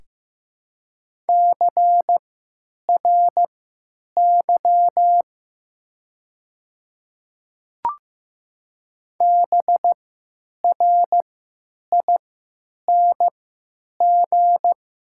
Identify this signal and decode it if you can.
Morse code, heard as a keyed tone, about 15 wpm, spaced as for 9 wpm, 700 Hz. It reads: CRY BRING
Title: 3x Character Spacing - Top 500 Words - 15wpm